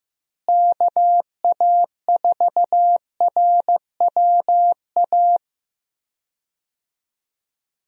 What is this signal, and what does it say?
Morse code, heard as a keyed tone, about 16 wpm, 700 Hz. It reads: KA4RWA
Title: USA Call Signs - 6 Characters - Repeated 2x - 15wpm